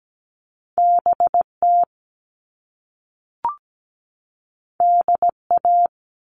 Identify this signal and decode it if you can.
Morse code, heard as a keyed tone, about 17 wpm, 700 Hz. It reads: BT DA
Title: All Cut Numbers - 2-Digit 17wpm